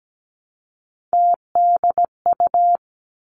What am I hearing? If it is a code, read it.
Morse code, heard as a keyed tone, about 17 wpm, 700 Hz. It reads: TDU
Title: Three Challenging Letter-Number Combinations 17wpm